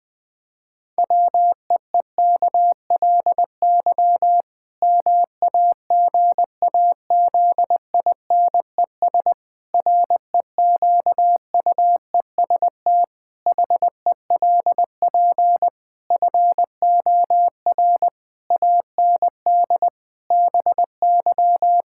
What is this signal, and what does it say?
Morse code, heard as a keyed tone, about 20 wpm, 700 Hz. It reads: WEEKLY MAGAZINES REQUEST HELP FOR AND BY
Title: Pangram Challenge v2 - 20wpm